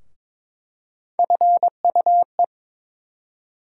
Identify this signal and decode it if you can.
Morse code, heard as a keyed tone, about 22 wpm, 700 Hz. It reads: FUE
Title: USA Call Sign Postfixes - 3 Letters 22wpm